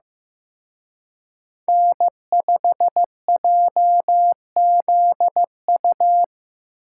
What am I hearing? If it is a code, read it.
Morse code, heard as a keyed tone, about 18 wpm, 700 Hz. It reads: N5JZU
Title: Speed Racing - USA 1x3 Call Signs 15wpm